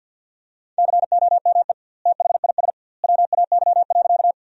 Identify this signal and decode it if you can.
Morse code, heard as a keyed tone, about 50 wpm, 700 Hz. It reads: COME THIS WAY.
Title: 2x Word-Spacing - Sentences from Top 100 Words - 50wpm